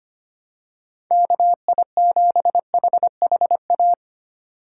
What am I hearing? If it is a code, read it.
Morse code, heard as a keyed tone, about 25 wpm, 700 Hz. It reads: KI7HHA
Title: USA Call Signs - 6 Characters - 2x Character Spacing - 25wpm